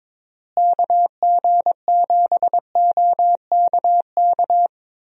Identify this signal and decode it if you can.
Morse code, heard as a keyed tone, about 22 wpm, 700 Hz. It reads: KG7OKK
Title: USA Call Signs - 6 Characters - Repeated 2x - 22wpm